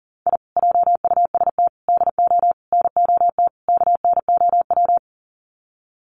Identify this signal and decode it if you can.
Morse code, heard as a keyed tone, about 40 wpm, 700 Hz. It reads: I JUST DO NOT KNOW
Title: Sets of 5 Words Encore 40wpm